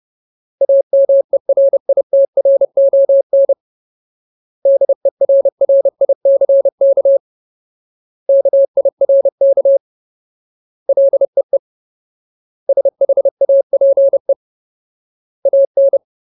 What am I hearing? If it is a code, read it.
Morse code, heard as a keyed tone, about 30 wpm, 550 Hz. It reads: AMERITRON DERRICK KIRK LEE SHAPE AN
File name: Ultimate Word List / 30WPM Ultimate